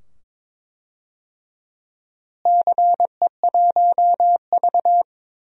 Morse code, heard as a keyed tone, about 22 wpm, 700 Hz. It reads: CE1V
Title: DXCC-100 Call Signs - 4 Characters - Repeated 2x - 22wpm